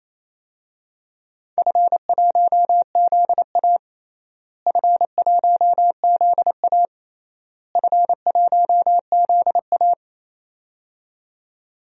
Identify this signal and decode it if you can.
Morse code, heard as a keyed tone, about 28 wpm, 700 Hz. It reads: F1ZA F1ZA F1ZA
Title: DXCC-100 Call Signs - 4 Characters - Repeated 3x - 28wpm